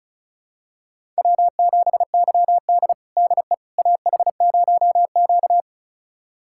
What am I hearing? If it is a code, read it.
Morse code, heard as a keyed tone, about 36 wpm, 700 Hz. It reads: W7YD DE AH0Q
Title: USA Callsign DE Callsign - 4 Characters - 35wpm